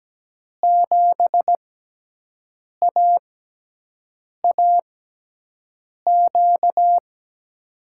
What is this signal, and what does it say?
Morse code, heard as a keyed tone, about 17 wpm, spaced as for 6 wpm, 700 Hz. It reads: 7AAQ
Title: USA Call Signs - 6 Characters - 6x Character Spacing - 17wpm